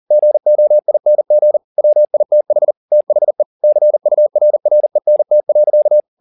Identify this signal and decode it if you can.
Morse code, heard as a keyed tone, about 40 wpm, 600 Hz. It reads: GOING WITH THE CURRENT.
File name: the old man and the sea 2 40wpm